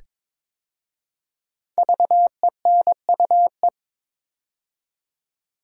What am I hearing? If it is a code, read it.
Morse code, heard as a keyed tone, about 22 wpm, 700 Hz. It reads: VENUE
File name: 5 Letter Words - First 3 Letters Given - 22wpm